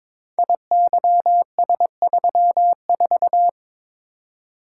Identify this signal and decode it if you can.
Morse code, heard as a keyed tone, about 22 wpm, 700 Hz. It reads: IYS34
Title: Code Groups - 5 Letter:Numbers - 22wpm